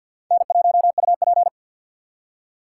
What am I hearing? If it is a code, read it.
Morse code, heard as a keyed tone, about 52 wpm, 700 Hz. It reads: N1UP
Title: USA Call Signs - 4 Characters - 2x Character Spacing - 50wpm